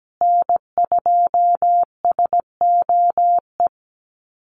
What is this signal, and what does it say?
Morse code, heard as a keyed tone, about 17 wpm, 700 Hz. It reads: N2SOE
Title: USA Call Signs Repeated 2x 17wpm